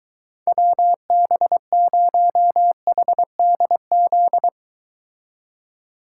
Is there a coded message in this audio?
Yes: Morse code from 0.5 to 4.5 s, heard as a keyed tone, about 23 wpm, 700 Hz. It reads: WB0HDZ